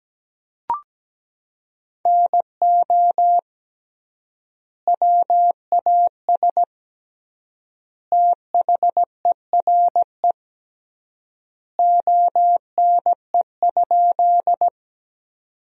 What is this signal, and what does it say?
Morse code, heard as a keyed tone, about 17 wpm, 700 Hz. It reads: NO WAS THERE ONE?
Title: 3x Word-Spacing - Sentences from Top 100 Words - 17wpm